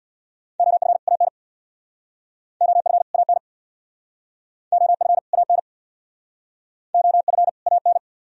Morse code, heard as a keyed tone, about 63 wpm, 700 Hz. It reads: OF AN OF AN OF AN OF AN